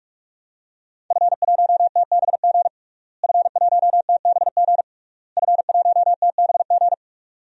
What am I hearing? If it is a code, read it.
Morse code, heard as a keyed tone, about 46 wpm, 700 Hz. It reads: F1TBG F1TBG F1TBG
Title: DXCC-100 Call Signs - 5 Characters - Repeated 3x - 45wpm